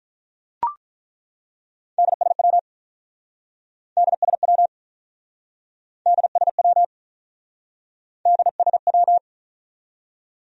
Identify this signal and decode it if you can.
Morse code, heard as a keyed tone, about 48 wpm, 700 Hz. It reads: DSW DSW DSW DSW